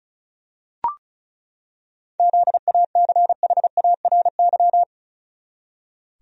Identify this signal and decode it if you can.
Morse code, heard as a keyed tone, about 36 wpm, 700 Hz. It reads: ZACHARY
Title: USA Common Names - 35wpm